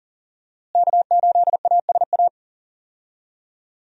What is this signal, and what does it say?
Morse code, heard as a keyed tone, about 39 wpm, 700 Hz. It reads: K8ASA
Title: USA Call Signs - 5 Characters - Repeated 3x - 40wpm